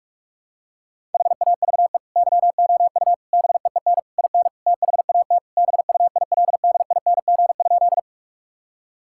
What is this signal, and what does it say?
Morse code, heard as a keyed tone, about 46 wpm, 700 Hz. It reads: HAVE YOU BEEN IN THAT BUILDING?
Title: Sentences from Top 1000 Words - Review Part 5 - 45wpm